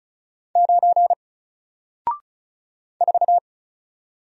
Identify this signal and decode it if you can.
Morse code, heard as a keyed tone, about 35 wpm, 700 Hz. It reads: 9 4
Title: Single Character (Letter:Number:Punctuation:Prosign) 35wpm